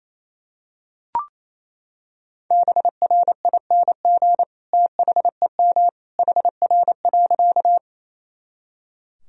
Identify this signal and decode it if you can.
Morse code, heard as a keyed tone, about 28 wpm, 700 Hz. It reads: BRING THEM HR.
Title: Sentences from Top 500 Words and Abbreviations 28wpm